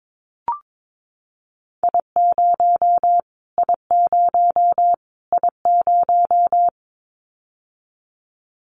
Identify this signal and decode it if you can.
Morse code, heard as a keyed tone, about 22 wpm, 700 Hz. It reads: I0 I0 I0
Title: Callsign Prefixes for Top 25 DXCC - Repeated 3x - 22wpm